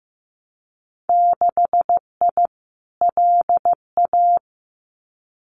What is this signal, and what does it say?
Morse code, heard as a keyed tone, about 15 wpm, 700 Hz. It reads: 6I LA
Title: Winter Field Day Exchanges Encore 15wpm